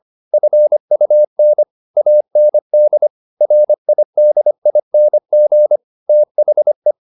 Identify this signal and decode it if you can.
Morse code, heard as a keyed tone, about 25 wpm, 600 Hz. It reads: FUN AND RIDING THE